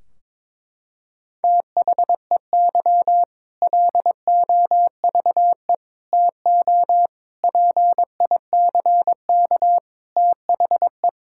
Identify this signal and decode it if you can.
Morse code, heard as a keyed tone, about 22 wpm, 700 Hz. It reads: THEY LOVE TO PICK THE
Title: Sentences from Top 1600 Words 22wpm